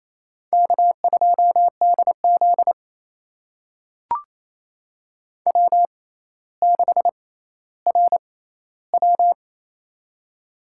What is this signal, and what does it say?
Morse code, heard as a keyed tone, about 28 wpm, spaced as for 14 wpm, 700 Hz. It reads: K2DZ W6RW
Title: USA Call Signs - 4 Characters - 6x Character Spacing - 28wpm